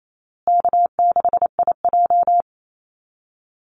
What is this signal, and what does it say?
Morse code, heard as a keyed tone, about 28 wpm, 700 Hz. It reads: K6IJ
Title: Easy Call Signs Repeated 2x - 28wpm